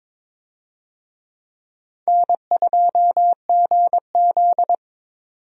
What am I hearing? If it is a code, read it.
Morse code, heard as a keyed tone, about 23 wpm, 700 Hz. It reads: N2GZ
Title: USA Call Signs - 4 Characters - 22wpm